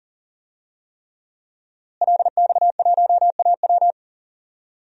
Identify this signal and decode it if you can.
Morse code, heard as a keyed tone, about 40 wpm, 700 Hz. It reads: LX1AW
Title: DXCC-100 Call Signs - 5 Characters - Repeated 3x - 40wpm